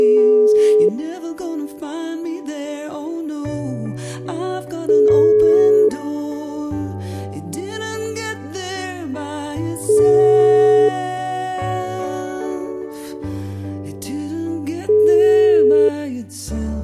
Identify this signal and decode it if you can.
Ringback tone (Europe, 425 Hz).